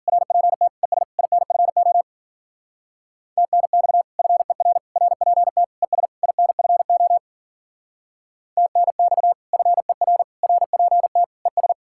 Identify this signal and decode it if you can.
Morse code, heard as a keyed tone, about 50 wpm, 700 Hz. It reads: RPT ES INFO TNX FER RPT ES INFO TNX FER RPT ES